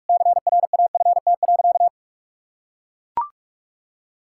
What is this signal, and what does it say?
Morse code, heard as a keyed tone, about 41 wpm, 700 Hz. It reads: CRAFT.
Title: Sentences from Top 1000 Words - Review Part 6 - 45wpm